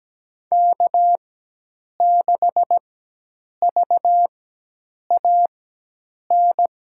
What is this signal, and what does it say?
Morse code, heard as a keyed tone, about 17 wpm, spaced as for 8 wpm, 700 Hz. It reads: K6VAN
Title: USA Call Signs - 5 Characters - 4x Character Spacing - 17wpm